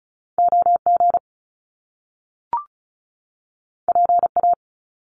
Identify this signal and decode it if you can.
Morse code, heard as a keyed tone, about 36 wpm, 700 Hz. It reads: OG PA
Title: Top Two Letter Combinations in Words 35wpm